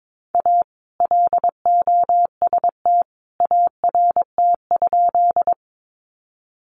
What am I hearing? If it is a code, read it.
Morse code, heard as a keyed tone, about 22 wpm, 700 Hz. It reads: A LOST ART?
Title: Sentences from Top 1400 Words 22wpm